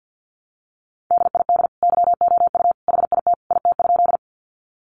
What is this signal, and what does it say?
Morse code, heard as a keyed tone, about 50 wpm, 700 Hz. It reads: DID YOU HIT IT?
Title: Sentences from Top 1200 Words 50wpm